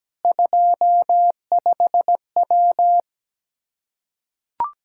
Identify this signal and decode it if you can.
Morse code, heard as a keyed tone, about 17 wpm, 700 Hz. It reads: 25W
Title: Practice QSO - Exchange 2 - Info - 17wpm